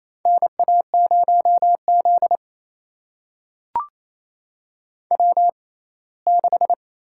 Morse code, heard as a keyed tone, about 27 wpm, spaced as for 19 wpm, 700 Hz. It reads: NA0Z W6
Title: USA Call Signs - 4 Characters - 6x Character Spacing - 28wpm